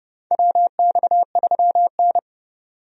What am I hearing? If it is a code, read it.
Morse code, heard as a keyed tone, about 30 wpm, 700 Hz. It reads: WX3N
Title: USA Call Signs - 4 Characters - 30wpm